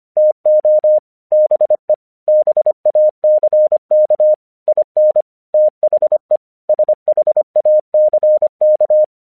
Morse code, heard as a keyed tone, about 25 wpm, 610 Hz. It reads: TO BE BACK IN THE SHACK
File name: 1 English 25wpm / AQC36